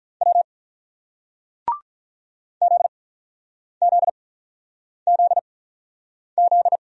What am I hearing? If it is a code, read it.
Morse code, heard as a keyed tone, about 48 wpm, spaced as for 9 wpm, 700 Hz. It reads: W ZZZZ